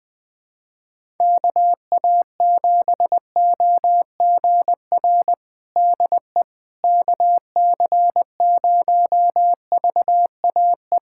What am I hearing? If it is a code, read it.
Morse code, heard as a keyed tone, about 20 wpm, 700 Hz. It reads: KA7OGR DE KC0VAE
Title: USA Callsign DE Callsign - 6 Characters - 20wpm